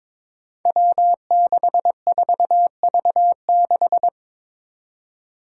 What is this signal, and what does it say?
Morse code, heard as a keyed tone, about 22 wpm, 700 Hz. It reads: W64V6
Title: Five Challenging Letter-Number Combinations 22wpm